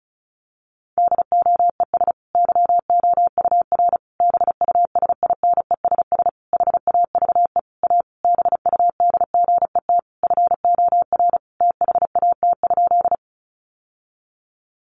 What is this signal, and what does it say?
Morse code, heard as a keyed tone, about 34 wpm, 700 Hz. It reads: DOES YOUR BUSINESS HAVE A BUDGET FOR THAT?
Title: Sentences from Top 1200 Words 35wpm